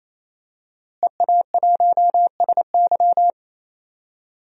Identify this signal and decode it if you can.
Morse code, heard as a keyed tone, about 27 wpm, 700 Hz. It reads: EA1SY
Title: DXCC-100 Call Signs - 5 Characters - Repeated 3x - 28wpm